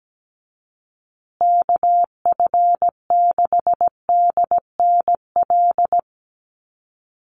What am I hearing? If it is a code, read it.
Morse code, heard as a keyed tone, about 17 wpm, 700 Hz. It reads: KF6DNL
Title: USA Call Signs Repeated 3x 17wpm